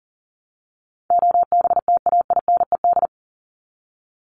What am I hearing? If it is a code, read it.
Morse code, heard as a keyed tone, about 40 wpm, 700 Hz. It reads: OBTAINED
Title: ED Suffix - Top 100 Words with Base Words 40wpm